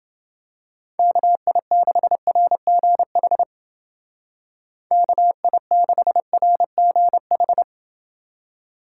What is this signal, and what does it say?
Morse code, heard as a keyed tone, about 28 wpm, 700 Hz. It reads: KI6RGH KI6RGH